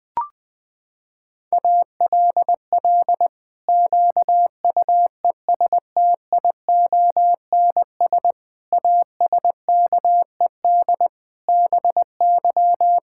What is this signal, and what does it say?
Morse code, heard as a keyed tone, about 20 wpm, 700 Hz. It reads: ALL QUESTIONS ASKED BY
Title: Pangram Challenge v2 - 20wpm